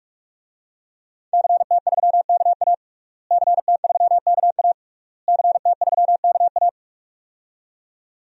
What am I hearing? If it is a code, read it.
Morse code, heard as a keyed tone, about 46 wpm, 700 Hz. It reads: CT3KA CT3KA CT3KA
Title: DXCC-100 Call Signs - 5 Characters - Repeated 3x - 45wpm